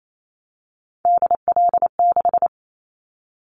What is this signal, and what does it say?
Morse code, heard as a keyed tone, about 28 wpm, 700 Hz. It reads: DL6
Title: Callsign Prefixes for Top 25 DXCC - 28wpm